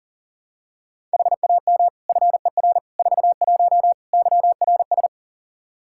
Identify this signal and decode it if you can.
Morse code, heard as a keyed tone, about 40 wpm, 700 Hz. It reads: HAM FER 41 YRS
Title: Practice QSO - Exchange 3 - More Info - 40wpm